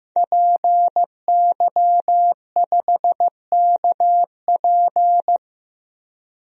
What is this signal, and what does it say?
Morse code, heard as a keyed tone, about 15 wpm, 700 Hz. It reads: PY5KP
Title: DXCC-100 Call Signs - 5 Characters - 15wpm